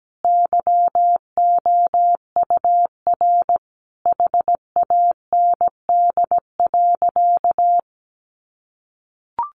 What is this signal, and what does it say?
Morse code, heard as a keyed tone, about 17 wpm, 700 Hz. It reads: YOUR HAND.
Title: Sentences from Top 1500 Words 17wpm